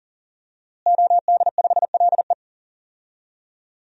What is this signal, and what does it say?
Morse code, heard as a keyed tone, about 40 wpm, 700 Hz. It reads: OD5LE
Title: DXCC-100 Call Signs - 5 Characters - Repeated 2x - 40wpm